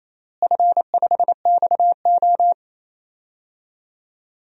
Morse code, heard as a keyed tone, about 28 wpm, 700 Hz. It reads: F5XO
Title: DXCC-100 Call Signs - 4 Characters - Repeated 2x - 28wpm